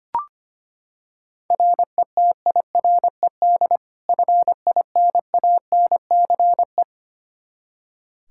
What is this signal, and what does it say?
Morse code, heard as a keyed tone, about 25 wpm, 700 Hz. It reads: RETIRED FINANCE
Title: K1USN SST QSO Practice 25wpm